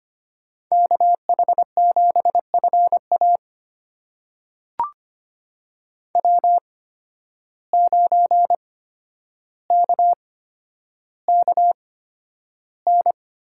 Morse code, heard as a keyed tone, about 25 wpm, spaced as for 11 wpm, 700 Hz. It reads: KH7FA W9KKN
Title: USA Call Signs - 5 Characters - 8x Character Spacing - 25wpm